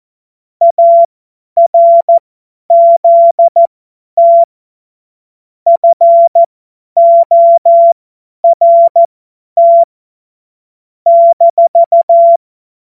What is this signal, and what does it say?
Morse code, heard as a keyed tone, about 14 wpm, spaced as for 10 wpm, 680 Hz. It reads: ARZT FORT -
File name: Morsekurs / Witze / Witz 19 März 2024 50 F70